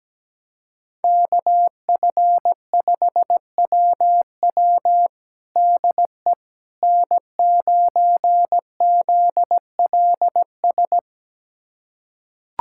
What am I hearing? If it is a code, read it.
Morse code, heard as a keyed tone, about 17 wpm, 700 Hz. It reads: KF5WW DE N9ZLS
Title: USA Callsign DE Callsign - 5 Characters - 17wpm